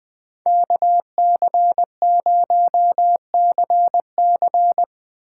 Morse code, heard as a keyed tone, about 20 wpm, 700 Hz. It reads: KC0CC
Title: USA Call Signs - 5 Characters - 3x Character Spacing - 20wpm